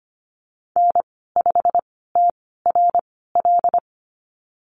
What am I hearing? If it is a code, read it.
Morse code, heard as a keyed tone, about 25 wpm, spaced as for 16 wpm, 700 Hz. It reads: N5TRL
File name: Lesson 15g - Number 5 - Call Signs 25wpm